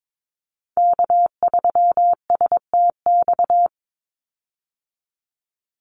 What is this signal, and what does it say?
Morse code, heard as a keyed tone, about 22 wpm, 700 Hz. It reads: K3STX
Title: Call Signs Repeated 3x - 22wpm